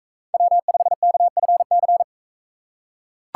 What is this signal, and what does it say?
Morse code, heard as a keyed tone, about 42 wpm, 700 Hz. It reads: W5KFC